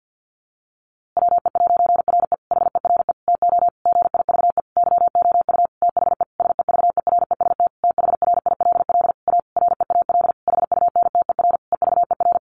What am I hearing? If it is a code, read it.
Morse code, heard as a keyed tone, about 50 wpm, 700 Hz. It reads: WE'RE HERE TO GIVE YOU THE SEVEREST THRILL A REAL HUNTER EVER
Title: A Sound Of Thunder 50wpm